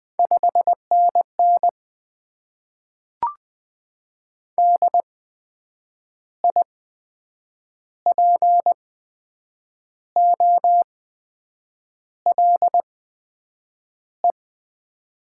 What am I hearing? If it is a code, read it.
Morse code, heard as a keyed tone, about 20 wpm, spaced as for 7 wpm, 700 Hz. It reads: 5NN DIPOLE
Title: Top 90 QSO Elements - 8x Character Spacing - 20wpm